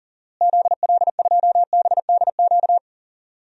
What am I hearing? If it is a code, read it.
Morse code, heard as a keyed tone, about 40 wpm, 700 Hz. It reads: ZL2BDQ
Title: DXCC-100 Call Signs - 6 Characters - Repeated 2x - 40wpm